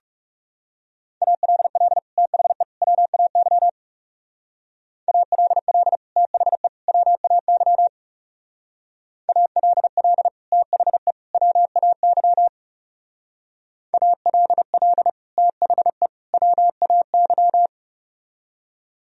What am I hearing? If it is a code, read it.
Morse code, heard as a keyed tone, about 36 wpm, 700 Hz. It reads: ALL THE WAY ALL THE WAY ALL THE WAY ALL THE WAY
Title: Speed-Racing - Sets of 3 Words 30wpm